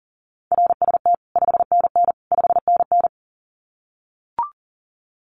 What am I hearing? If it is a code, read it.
Morse code, heard as a keyed tone, about 40 wpm, 700 Hz. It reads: RST 5NN 5NN
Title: Common QSO Phrases Encore 40wpm